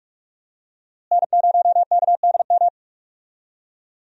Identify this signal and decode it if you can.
Morse code, heard as a keyed tone, about 43 wpm, 700 Hz. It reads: N0KDM
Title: USA Call Signs - 5 Characters - Repeated 2x - 45wpm